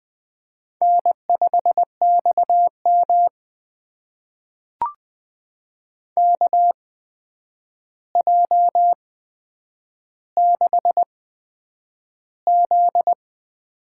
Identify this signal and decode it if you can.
Morse code, heard as a keyed tone, about 21 wpm, spaced as for 9 wpm, 700 Hz. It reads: N5XM KJ6Z
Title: USA Call Signs - 8x Character Spacing - 20wpm